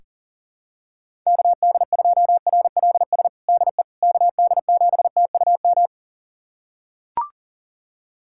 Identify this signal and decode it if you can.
Morse code, heard as a keyed tone, about 37 wpm, 700 Hz. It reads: KD2RLS DE KD7TUM
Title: USA Callsign DE Callsign - 6 Characters - 40wpm